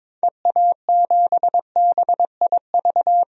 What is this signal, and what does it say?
Morse code, heard as a keyed tone, about 22 wpm, 700 Hz. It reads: EA7BIV
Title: DXCC-100 Call Signs - 6 Characters - 22wpm